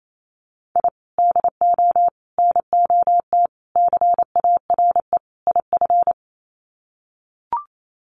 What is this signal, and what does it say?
Morse code, heard as a keyed tone, about 28 wpm, 700 Hz. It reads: I DO NOT CARE IF
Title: Sets of 5 Words 28wpm